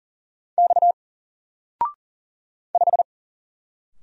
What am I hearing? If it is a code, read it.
Morse code, heard as a keyed tone, about 40 wpm, 700 Hz. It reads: X 5
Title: Single Character (Letter:Number:Punctuation:Prosign) 40wpm